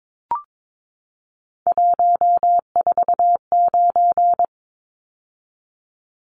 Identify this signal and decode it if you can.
Morse code, heard as a keyed tone, about 22 wpm, 700 Hz. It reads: 149
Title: Numbers - 3-Digit 22wpm